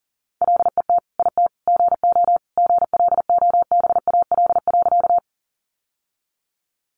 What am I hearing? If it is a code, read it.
Morse code, heard as a keyed tone, about 40 wpm, 700 Hz. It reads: LET IT GO GLOBAL.
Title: Sentences from Top 1200 Words 40wpm